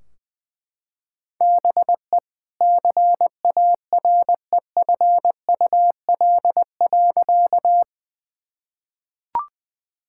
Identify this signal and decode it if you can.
Morse code, heard as a keyed tone, about 20 wpm, 700 Hz. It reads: BE CAREFUL.
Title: Sentences from Top 1600 Words 20wpm